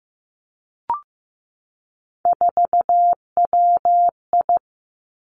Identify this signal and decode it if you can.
Morse code, heard as a keyed tone, about 15 wpm, 700 Hz. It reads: 4WI
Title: Three Challenging Letter-Number Combinations 15wpm